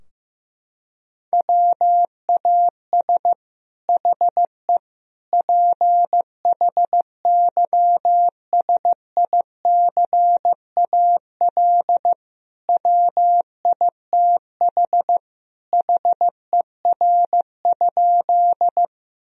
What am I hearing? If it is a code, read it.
Morse code, heard as a keyed tone, about 15 wpm, 700 Hz. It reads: WAS HE PHYSICAL WITH HER?
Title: Sentences from Top 1000 Words - Review Part 4 - 15wpm